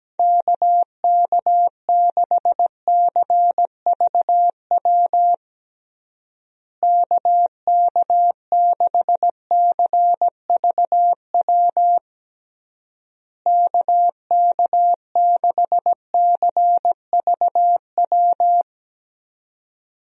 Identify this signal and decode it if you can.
Morse code, heard as a keyed tone, about 17 wpm, 700 Hz. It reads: KK6CVW KK6CVW KK6CVW
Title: USA Call Signs - 6 Characters - Repeated 3x - 17wpm